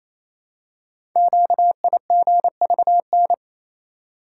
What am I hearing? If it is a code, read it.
Morse code, heard as a keyed tone, about 27 wpm, 700 Hz. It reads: QIGVN
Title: Code Groups - 5 Letters - 28wpm